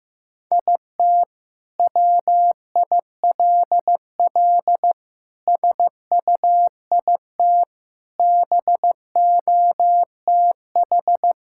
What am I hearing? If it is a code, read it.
Morse code, heard as a keyed tone, about 15 wpm, 700 Hz. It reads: IT WILL SUIT BOTH
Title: Sentences from Top 1600 Words 15wpm